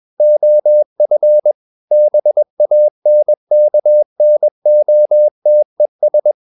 Morse code, heard as a keyed tone, about 21 wpm, 600 Hz. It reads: OF BANKNOTES
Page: CW med Gnister